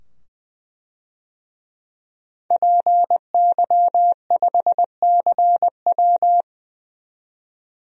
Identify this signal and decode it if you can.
Morse code, heard as a keyed tone, about 20 wpm, 700 Hz. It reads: PY5CW
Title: DXCC-100 Call Signs - 5 Characters - 20wpm